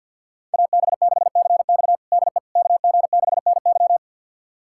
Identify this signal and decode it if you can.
Morse code, heard as a keyed tone, about 52 wpm, 700 Hz. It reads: AB6CX DE KG6NY
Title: USA Callsign DE Callsign - 5 Characters - 50wpm